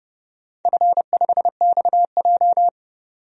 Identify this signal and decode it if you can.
Morse code, heard as a keyed tone, about 30 wpm, 700 Hz. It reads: F5XJ
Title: DXCC-100 Call Signs - 4 Characters - 30wpm